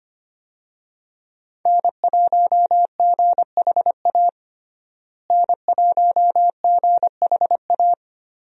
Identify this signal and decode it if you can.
Morse code, heard as a keyed tone, about 25 wpm, 700 Hz. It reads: N1GHA N1GHA
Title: USA Call Signs - 5 Characters - Repeated 2x - 25wpm